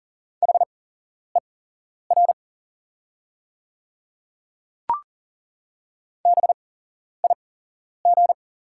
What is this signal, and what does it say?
Morse code, heard as a keyed tone, about 38 wpm, spaced as for 11 wpm, 690 Hz. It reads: HER BIG
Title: 8x Character Spacing - Top 300 Words - 40wpm